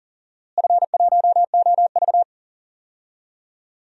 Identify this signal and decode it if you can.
Morse code, heard as a keyed tone, about 40 wpm, 700 Hz. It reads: F1OV
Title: DXCC-100 Call Signs - 4 Characters - Repeated 3x - 40wpm